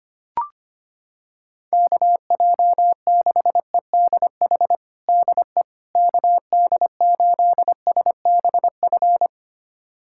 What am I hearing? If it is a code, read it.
Morse code, heard as a keyed tone, about 25 wpm, 700 Hz. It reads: KJ6EDH DE KD8SBF
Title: USA Callsign DE Callsign - 6 Characters - 25wpm